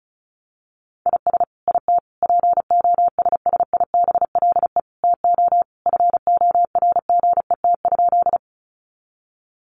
Morse code, heard as a keyed tone, about 34 wpm, 700 Hz. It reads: IS IT POSSIBLE TO FORGET?
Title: Sentences from Top 1200 Words 35wpm